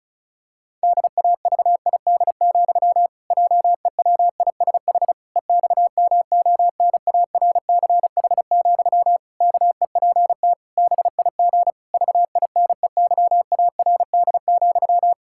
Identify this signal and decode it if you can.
Morse code, heard as a keyed tone, about 36 wpm, 700 Hz. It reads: DAVID, JEWISH EXMONARCH, KEPT BIG VINEYARD,
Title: Pangram Challenge v2 - 35wpm